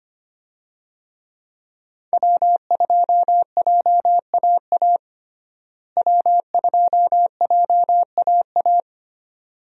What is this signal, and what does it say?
Morse code, heard as a keyed tone, about 25 wpm, 700 Hz. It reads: W2JAA W2JAA
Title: USA Call Signs - 5 Characters - Repeated 2x - 25wpm